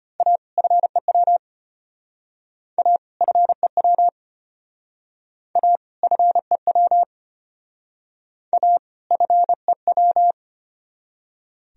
Morse code, heard as a keyed tone, about 30 wpm, 700 Hz. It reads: A FEW A FEW A FEW A FEW
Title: Speed-Racing - Sets of 2 Words 25wpm